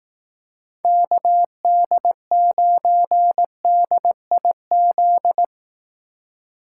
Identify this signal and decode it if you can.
Morse code, heard as a keyed tone, about 18 wpm, 700 Hz. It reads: KD9DIZ
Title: Speed Racing - USA 2x3 Call Signs 15wpm